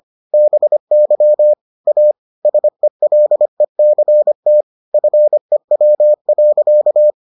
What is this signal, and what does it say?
Morse code, heard as a keyed tone, about 25 wpm, 600 Hz. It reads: BY A SELECT FEW.